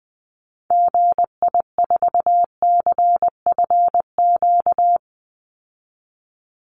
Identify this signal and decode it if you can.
Morse code, heard as a keyed tone, about 20 wpm, 700 Hz. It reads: GI4CFQ
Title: Call Signs Repeated 3x - 20wpm